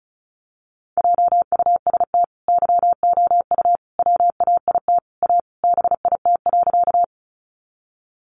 Morse code, heard as a keyed tone, about 34 wpm, 700 Hz. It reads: JUST YOU WAIT A BIT.
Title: Sentences from Top 900 Words 35wpm